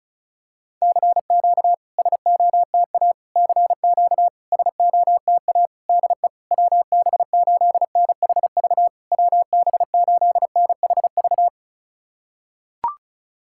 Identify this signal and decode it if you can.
Morse code, heard as a keyed tone, about 36 wpm, 690 Hz. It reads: CQ SOTA CQ SOTA DE WB8NHV WB8NHV
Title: SOTA QSO Practice 35wpm